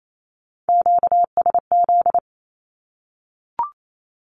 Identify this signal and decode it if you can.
Morse code, heard as a keyed tone, about 28 wpm, 700 Hz. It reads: QSZ
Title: Q-Codes Repeated 3x 28wpm